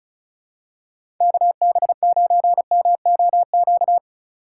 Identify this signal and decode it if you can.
Morse code, heard as a keyed tone, about 36 wpm, 700 Hz. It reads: KB9MOQ
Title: USA Call Signs - 6 Characters - 2x Character Spacing - 35wpm